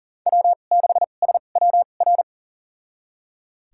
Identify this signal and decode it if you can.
Morse code, heard as a keyed tone, about 40 wpm, spaced as for 29 wpm, 700 Hz. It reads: W6SWR
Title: USA Call Signs - 5 Characters - 2x Character Spacing - 40wpm